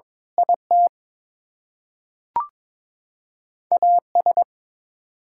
Morse code, heard as a keyed tone, about 23 wpm, 700 Hz. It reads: IT AS
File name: Build A Phrase - 2-Word Phrases - 22wpm